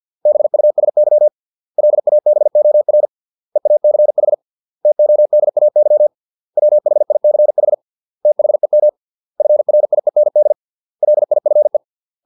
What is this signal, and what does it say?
Morse code, heard as a keyed tone, about 50 wpm, 600 Hz. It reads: BUSY LABOR EACH TODAY WHICH THEM FRIEND LIFE
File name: 632 words / 632 Words 50WPM